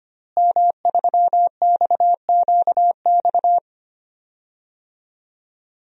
Morse code, heard as a keyed tone, about 25 wpm, 700 Hz. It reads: M3XQX
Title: DXCC-100 Call Signs - 5 Characters - Repeated 3x - 25wpm